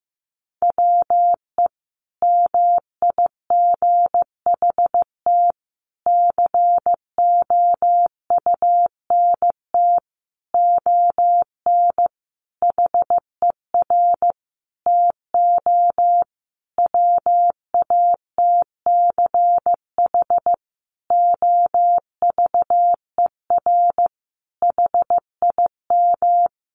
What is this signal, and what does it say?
Morse code, heard as a keyed tone, about 15 wpm, 700 Hz. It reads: WE MIGHT COUNT ON HER TO WATCH OVER HIM
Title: Sentences from Top 1500 Words 15wpm